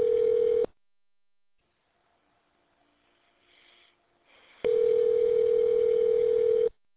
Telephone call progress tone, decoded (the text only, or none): ringback tone